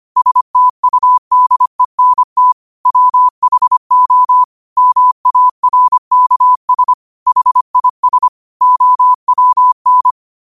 Morse code, heard as a keyed tone, about 25 wpm, 1000 Hz. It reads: STUDENT WHO MARKS HIS OWN
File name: qotdfiles / 25wpm